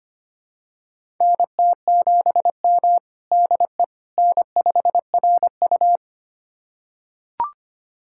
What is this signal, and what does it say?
Morse code, heard as a keyed tone, about 25 wpm, 700 Hz. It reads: NT7M DE N5RU
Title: USA Callsign DE Callsign - 4 Characters - 25wpm